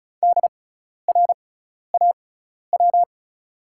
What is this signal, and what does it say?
Morse code, heard as a keyed tone, about 36 wpm, spaced as for 12 wpm, 700 Hz. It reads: DRAW
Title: 6x Character Spacing - Top 300 Words - 35wpm